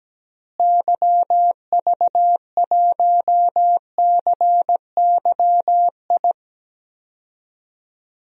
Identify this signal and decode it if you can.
Morse code, heard as a keyed tone, about 17 wpm, 700 Hz. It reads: YV1CYI
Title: DXCC-100 Call Signs - 6 Characters - Repeated 2x - 17wpm